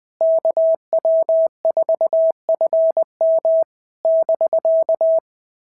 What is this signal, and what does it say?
Morse code, heard as a keyed tone, about 21 wpm, 650 Hz. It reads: KW4FM <BK>
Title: POTA QSO Practice - 20wpm